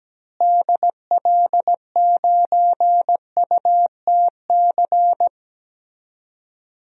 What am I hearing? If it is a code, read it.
Morse code, heard as a keyed tone, about 17 wpm, 700 Hz. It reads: DL9UTC
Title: DXCC-100 Call Signs - 6 Characters - Repeated 3x - 17wpm